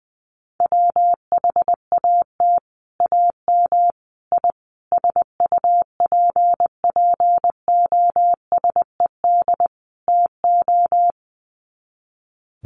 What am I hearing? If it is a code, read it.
Morse code, heard as a keyed tone, about 20 wpm, 700 Hz. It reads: WHAT AM I SUPPOSED TO
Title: Sets of 5 Words Encore2 20wpm